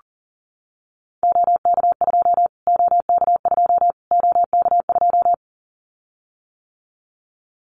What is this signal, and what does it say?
Morse code, heard as a keyed tone, about 40 wpm, 700 Hz. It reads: OK2 OK2 OK2
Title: Callsign Prefixes for Top 25 DXCC - Repeated 3x - 40wpm